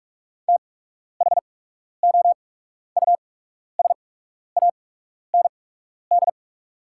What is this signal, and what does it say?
Morse code, heard as a keyed tone, about 45 wpm, spaced as for 12 wpm, 700 Hz. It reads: THOUSAND
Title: 8x Character Spacing - Top 500 Words - 45wpm